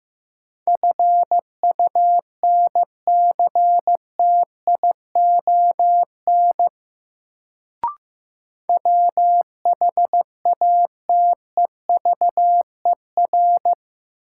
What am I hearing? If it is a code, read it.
Morse code, heard as a keyed tone, about 15 wpm, 700 Hz. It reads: FUNCTION WHATEVER
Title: Sentences from Top 1100 Words - New Word Drill - 15wpm